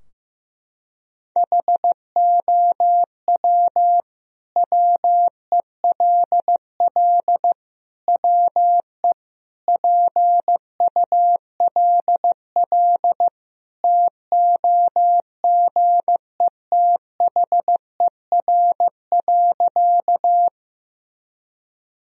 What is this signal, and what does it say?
Morse code, heard as a keyed tone, about 15 wpm, 700 Hz. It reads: HOW WELL WE PULL TOGETHER.
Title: Sentences from Top 1000 Words - Review Part 2 - 15wpm